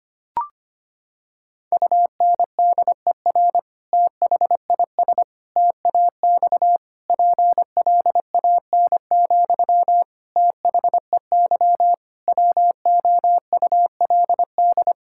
Morse code, heard as a keyed tone, about 25 wpm, 700 Hz. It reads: UNDER THIS TAX PLAN, THEY WOULD
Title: Sentences from Top 1000 Words - Review Part 5 - 25wpm